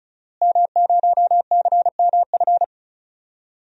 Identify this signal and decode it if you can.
Morse code, heard as a keyed tone, about 36 wpm, 700 Hz. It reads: M0CMF
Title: DXCC-100 Call Signs - 5 Characters - Repeated 2x - 35wpm